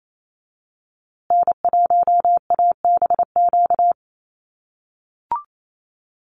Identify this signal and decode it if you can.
Morse code, heard as a keyed tone, about 28 wpm, 700 Hz. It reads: N1ABQ
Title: USA Call Signs Repeated 3x 28wpm